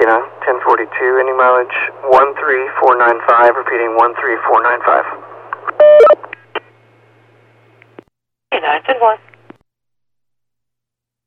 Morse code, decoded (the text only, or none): none